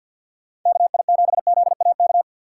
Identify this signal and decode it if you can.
Morse code, heard as a keyed tone, about 59 wpm, 700 Hz. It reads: KI7ZAK
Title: Speed Racing - USA 2x3 Call Signs 50wpm